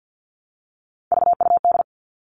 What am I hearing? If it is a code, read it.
Morse code, heard as a keyed tone, about 50 wpm, 700 Hz. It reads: VUD
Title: Three Challenging Letter-Number Combinations 50wpm